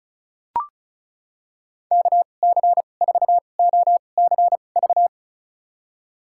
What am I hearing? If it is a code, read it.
Morse code, heard as a keyed tone, about 36 wpm, spaced as for 25 wpm, 700 Hz. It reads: KC4OCV
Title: USA Call Signs - 6 Characters - 2x Character Spacing - 35wpm